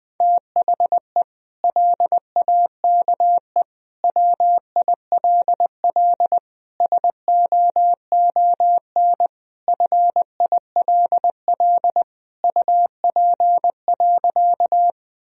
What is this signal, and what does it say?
Morse code, heard as a keyed tone, about 21 wpm, 700 Hz. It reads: THE LAKE WILL SOON FILL UP.
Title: Sentences from Top 1000 Words - Review Part 3 - 20wpm